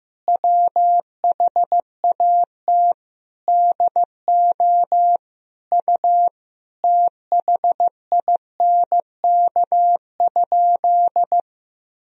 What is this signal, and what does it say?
Morse code, heard as a keyed tone, about 15 wpm, 700 Hz. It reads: WHAT DO U THINK?
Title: Sentences from Top 200 Words and Abbreviations 15wpm